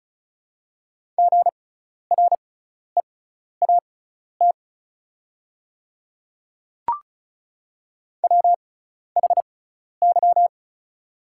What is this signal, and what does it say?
Morse code, heard as a keyed tone, about 36 wpm, spaced as for 12 wpm, 700 Hz. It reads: GREAT WHY
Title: 6x Character Spacing - Top 200 Words - 35wpm